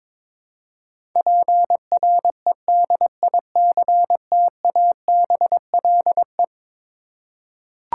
Morse code, heard as a keyed tone, about 22 wpm, 700 Hz. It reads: PREDICTABLE
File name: practice / 22wpm / Eleven Letter Words - 22wpm